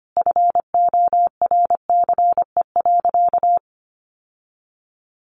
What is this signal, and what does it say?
Morse code, heard as a keyed tone, about 25 wpm, 700 Hz. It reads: FORCE.
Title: Sentences from Top 1200 Words 25wpm